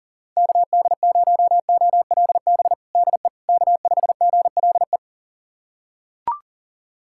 Morse code, heard as a keyed tone, about 40 wpm, 700 Hz. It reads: KD0OLB DE K5GLE
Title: USA Callsign DE Callsign 40wpm